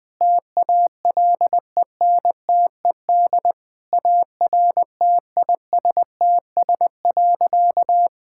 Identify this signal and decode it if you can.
Morse code, heard as a keyed tone, about 20 wpm, 700 Hz. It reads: TALENTED ARTISTS.
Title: Sentences with 8-Word Phrases - 20wpm